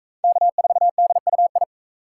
Morse code, heard as a keyed tone, about 43 wpm, 700 Hz. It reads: K4DUI